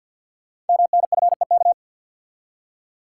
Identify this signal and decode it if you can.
Morse code, heard as a keyed tone, about 48 wpm, 690 Hz. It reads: MNFEK
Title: Code Groups - 5 Letters - 50wpm